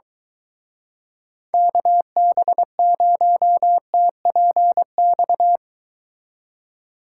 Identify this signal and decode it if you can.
Morse code, heard as a keyed tone, about 23 wpm, 700 Hz. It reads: KB0TPX